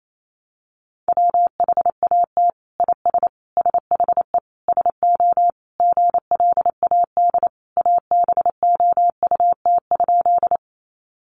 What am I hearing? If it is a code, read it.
Morse code, heard as a keyed tone, about 28 wpm, 700 Hz. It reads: WHAT IS SHE SO GLAD ABOUT?
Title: Sentences from Top 800 Words 28wpm